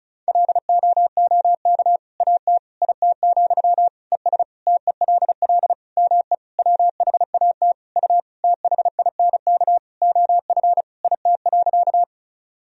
Morse code, heard as a keyed tone, about 36 wpm, 700 Hz. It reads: LOOK AT IT, ES TELL ME WHAT U THINK OF IT.
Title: Sentences from Top 200 Words and Abbreviations 35wpm